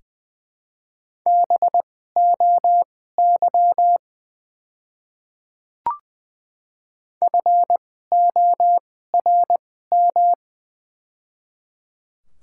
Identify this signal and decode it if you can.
Morse code, heard as a keyed tone, about 20 wpm, spaced as for 14 wpm, 700 Hz. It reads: BOY FORM
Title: 2x Character Spacing - Top 500 Words - 20wpm